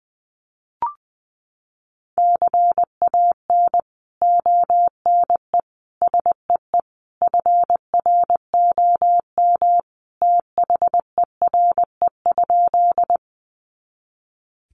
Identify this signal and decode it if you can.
Morse code, heard as a keyed tone, about 20 wpm, 700 Hz. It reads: CAN ONE SEE FROM THERE?
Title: Sentences from Top 100 Words 20wpm